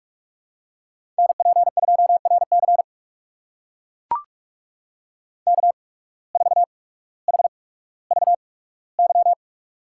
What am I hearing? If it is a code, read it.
Morse code, heard as a keyed tone, about 44 wpm, spaced as for 19 wpm, 700 Hz. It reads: NP2RC K4HVY
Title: USA Call Signs - 5 Characters - 8x Character Spacing - 45wpm